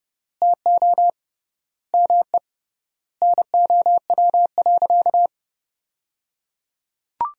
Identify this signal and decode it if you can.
Morse code, heard as a keyed tone, about 30 wpm, 700 Hz. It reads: TO ME NOW.
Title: 3x Word-Spacing - Sentences from Top 500 Words - 30wpm